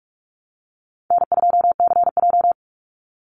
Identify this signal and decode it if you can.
Morse code, heard as a keyed tone, about 44 wpm, 700 Hz. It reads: N2CJ
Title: Call Signs - 45wpm